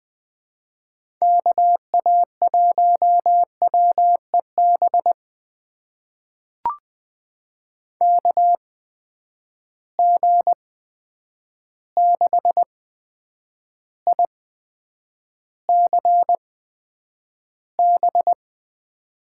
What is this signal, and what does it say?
Morse code, heard as a keyed tone, about 20 wpm, spaced as for 9 wpm, 700 Hz. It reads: KA1WEB KG6ICB
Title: USA Call Signs - 6 Characters - 8x Character Spacing - 20wpm